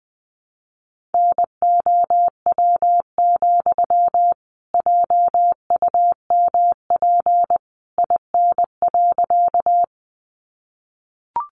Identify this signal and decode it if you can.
Morse code, heard as a keyed tone, about 20 wpm, 700 Hz. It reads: NOW, JUMP IN.
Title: Sentences from Top 1300 Words 20wpm